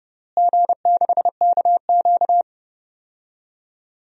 Morse code, heard as a keyed tone, about 29 wpm, 700 Hz. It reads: G6KQ
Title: DXCC-100 Call Signs - 4 Characters - Repeated 3x - 30wpm